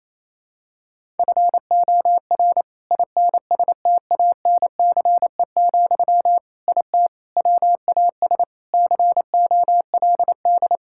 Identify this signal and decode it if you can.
Morse code, heard as a keyed tone, about 28 wpm, 700 Hz. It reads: FOR INSTANCE, IT WAS COLD
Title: Sentences from Top 1000 Words - Review Part 5 - 28wpm